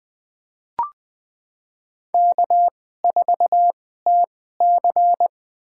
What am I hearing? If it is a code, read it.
Morse code, heard as a keyed tone, about 20 wpm, spaced as for 15 wpm, 700 Hz. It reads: K4TC
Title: USA Call Signs - 4 Characters - 2x Character Spacing - 20wpm